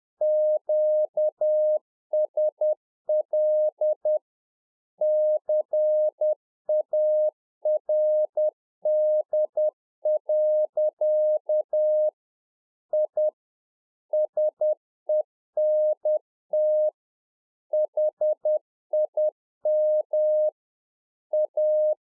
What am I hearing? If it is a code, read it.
Morse code, heard as a keyed tone, about 10 wpm, 610 Hz. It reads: QSL CARD. I SENT HIM A
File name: UNFOGETTABLE QSO 10WPM